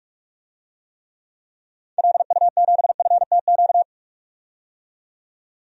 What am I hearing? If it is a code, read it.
Morse code, heard as a keyed tone, about 42 wpm, 700 Hz. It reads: LU7FTQ
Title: DXCC-100 Call Signs - 6 Characters - 45wpm